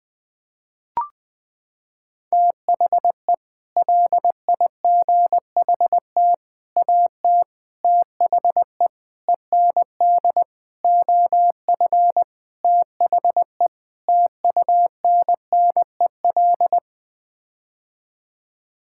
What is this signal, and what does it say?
Morse code, heard as a keyed tone, about 20 wpm, 700 Hz. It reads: THE LIGHT AT THE END OF THE TUNNEL
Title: Sets of 8 Words - 20wpm